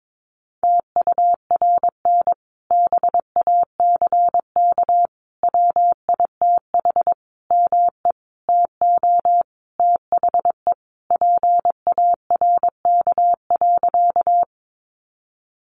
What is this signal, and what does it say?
Morse code, heard as a keyed tone, about 22 wpm, 700 Hz. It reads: TURN BACK WITH ME TO THE PARK.
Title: Sentences from Top 1300 Words 22wpm